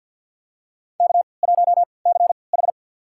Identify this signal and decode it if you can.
Morse code, heard as a keyed tone, about 53 wpm, spaced as for 29 wpm, 700 Hz. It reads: K1CH
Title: USA Call Signs - 4 Characters - 3x Character Spacing - 50wpm